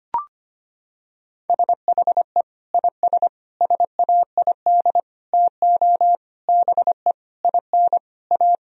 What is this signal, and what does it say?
Morse code, heard as a keyed tone, about 25 wpm, 700 Hz. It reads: SHE IS SAID TO BE IN A
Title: Sentences from Top 1000 Words - Review Part 2 - 25wpm